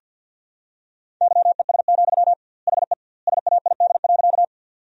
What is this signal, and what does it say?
Morse code, heard as a keyed tone, about 50 wpm, 700 Hz. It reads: YES, HE SAID.
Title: 2x Word-Spacing - Sentences from Top 500 Words - 50wpm